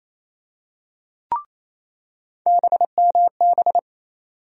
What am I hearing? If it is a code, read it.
Morse code, heard as a keyed tone, about 28 wpm, 700 Hz. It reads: BMB
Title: USA Call Sign Postfixes - 3 Letters 28wpm